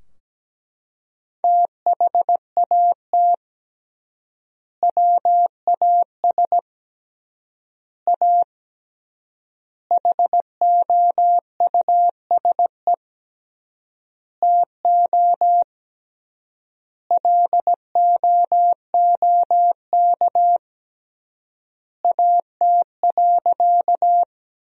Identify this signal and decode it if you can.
Morse code, heard as a keyed tone, about 17 wpm, 700 Hz. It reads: THAT WAS A HOUSE TO LOOK AT.
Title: 3x Word-Spacing - Sentences from Top 200 Words - 17wpm